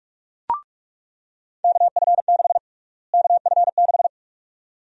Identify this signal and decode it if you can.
Morse code, heard as a keyed tone, about 46 wpm, 700 Hz. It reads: KF6 KF6
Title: USA Call Sign Prefixes - Repeated 2x - 45wpm